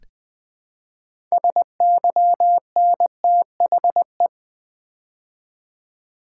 Morse code, heard as a keyed tone, about 20 wpm, 700 Hz. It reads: SYNTHE
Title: Build A Word - 3 Letters At A Time - 9-Letter Words - 20wpm